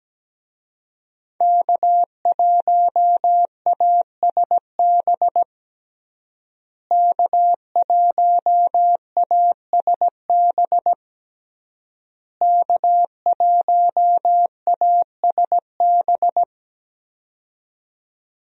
Morse code, heard as a keyed tone, about 17 wpm, 700 Hz. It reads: K1ASB K1ASB K1ASB
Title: USA Call Signs - 5 Characters - Repeated 3x - 17wpm